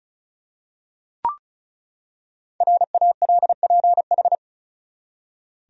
Morse code, heard as a keyed tone, about 36 wpm, 700 Hz. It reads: RALPH
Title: USA Common Names - Max 5 Letters - 35wpm